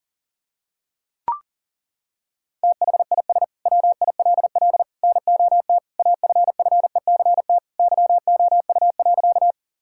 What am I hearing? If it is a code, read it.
Morse code, heard as a keyed tone, about 40 wpm, 700 Hz. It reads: THIS WILL NOT AFFECT YOU.
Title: Sentences from Top 1000 Words - Review Part 6 - 40wpm